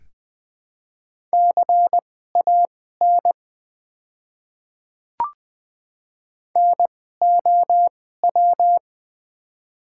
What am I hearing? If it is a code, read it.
Morse code, heard as a keyed tone, about 20 wpm, spaced as for 15 wpm, 700 Hz. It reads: CAN NOW
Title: 2x Character Spacing - Top 100 Words - 20wpm